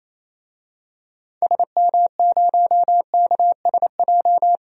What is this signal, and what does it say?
Morse code, heard as a keyed tone, about 28 wpm, 700 Hz. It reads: SM0KSJ
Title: DXCC-100 Call Signs - 6 Characters - 28wpm